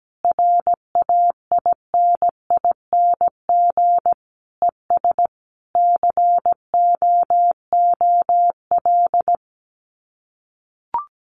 Morse code, heard as a keyed tone, about 17 wpm, 700 Hz. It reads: RAINING ES COOL
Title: Common QSO Phrases 17wpm